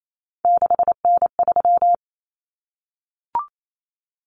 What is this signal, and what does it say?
Morse code, heard as a keyed tone, about 28 wpm, 700 Hz. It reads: 6N3
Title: Common Cut Numbers - 3-Digit 28wpm